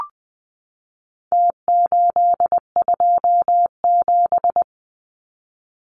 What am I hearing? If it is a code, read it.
Morse code, heard as a keyed tone, about 20 wpm, 700 Hz. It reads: T827
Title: Numbers - Contest Style - 4-Digit 20wpm